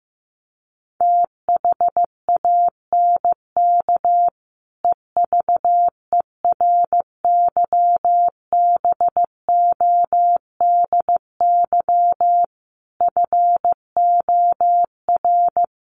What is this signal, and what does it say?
Morse code, heard as a keyed tone, about 15 wpm, 700 Hz. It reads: THANK EVERYBODY FOR
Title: Sentences from Top 1400 Words 15wpm